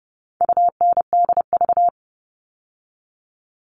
Morse code, heard as a keyed tone, about 30 wpm, 700 Hz. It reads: UNDV
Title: All Cut Numbers - 4-Digit 30wpm